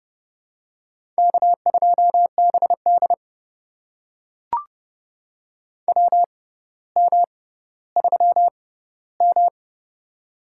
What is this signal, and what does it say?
Morse code, heard as a keyed tone, about 29 wpm, spaced as for 15 wpm, 700 Hz. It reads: K2BD WM3M
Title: USA Call Signs - 4 Characters - 6x Character Spacing - 30wpm